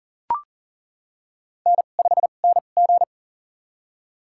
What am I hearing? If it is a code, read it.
Morse code, heard as a keyed tone, about 37 wpm, spaced as for 28 wpm, 700 Hz. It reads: N5NG
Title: USA Call Signs - 4 Characters - 2x Character Spacing - 40wpm